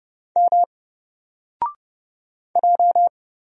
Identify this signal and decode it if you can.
Morse code, heard as a keyed tone, about 30 wpm, 700 Hz. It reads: M J
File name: Single-Letter-Number-30wpm